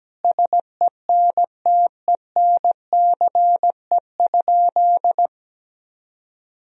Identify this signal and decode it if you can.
Morse code, heard as a keyed tone, about 17 wpm, 700 Hz. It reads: SENTENCE?
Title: Sentences from Top 1000 Words - Review Part 1 - 17wpm